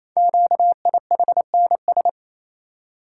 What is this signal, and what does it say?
Morse code, heard as a keyed tone, about 28 wpm, 700 Hz. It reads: QIHNS
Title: Code Groups - 5 Letters - 28wpm